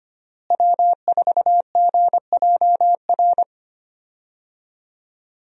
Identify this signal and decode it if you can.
Morse code, heard as a keyed tone, about 25 wpm, 700 Hz. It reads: W4GJR
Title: USA Call Signs - 5 Characters - Repeated 3x - 25wpm